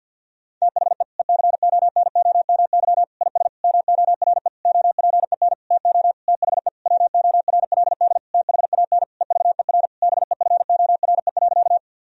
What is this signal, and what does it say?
Morse code, heard as a keyed tone, about 50 wpm, 700 Hz. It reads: THE ECONOMY IS MORE OPEN TO THE WORLD THAN EVER BEFORE.
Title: Sentences from Top 1000 Words - Review Part 3 - 50wpm